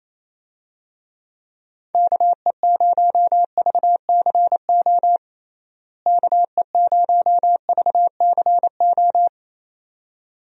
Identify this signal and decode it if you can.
Morse code, heard as a keyed tone, about 28 wpm, 700 Hz. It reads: KE0VCO KE0VCO
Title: USA Call Signs - 6 Characters - Repeated 2x - 28wpm